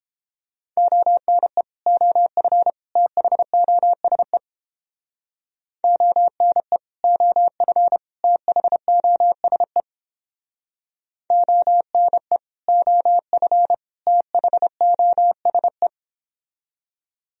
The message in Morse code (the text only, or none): ONE OF THOSE ONE OF THOSE ONE OF THOSE